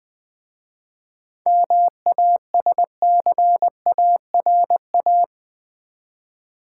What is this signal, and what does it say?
Morse code, heard as a keyed tone, about 20 wpm, 700 Hz. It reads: MASCARA
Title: Easy Words with Hints - 20wpm